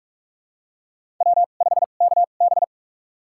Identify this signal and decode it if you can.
Morse code, heard as a keyed tone, about 44 wpm, spaced as for 32 wpm, 700 Hz. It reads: W5KB
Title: USA Call Signs - 4 Characters - 2x Character Spacing - 45wpm